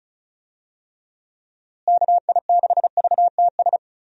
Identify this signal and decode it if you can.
Morse code, heard as a keyed tone, about 34 wpm, 700 Hz. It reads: KI6VTS
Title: USA Call Signs - 6 Characters - Repeated 2x - 35wpm